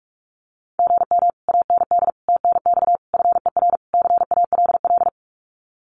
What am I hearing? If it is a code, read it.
Morse code, heard as a keyed tone, about 46 wpm, 700 Hz. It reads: GM AND TNX FER CALL
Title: Common QSO Phrases Encore 45wpm